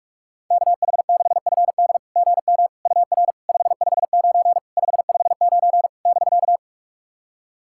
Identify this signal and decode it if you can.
Morse code, heard as a keyed tone, about 44 wpm, 690 Hz. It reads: KH6FD GM UR 559 559 <BK>
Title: SOTA QSO Practice 45wpm